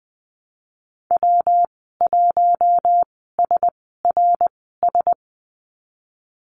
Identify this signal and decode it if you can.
Morse code, heard as a keyed tone, about 20 wpm, spaced as for 15 wpm, 700 Hz. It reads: W1SRS
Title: 154 Lesson 25g - Letter p - Call Signs 20wpm